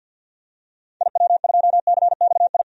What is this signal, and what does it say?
Morse code, heard as a keyed tone, about 47 wpm, 700 Hz. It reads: IW2CXI
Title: DXCC-100 Call Signs - 6 Characters - 50wpm